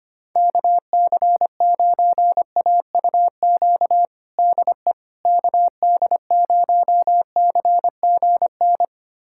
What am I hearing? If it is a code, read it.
Morse code, heard as a keyed tone, about 25 wpm, 700 Hz. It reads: KC9AUQ DE KD0CGN
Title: USA Callsign DE Callsign - 6 Characters - 25wpm